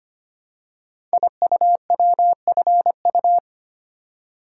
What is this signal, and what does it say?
Morse code, heard as a keyed tone, about 25 wpm, 700 Hz. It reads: IUWFU
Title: Code Groups - 5 Letters - 25wpm